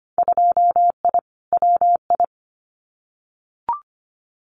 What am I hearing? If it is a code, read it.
Morse code, heard as a keyed tone, about 25 wpm, 700 Hz. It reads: 2I WI
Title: Winter Field Day Exchanges 25wpm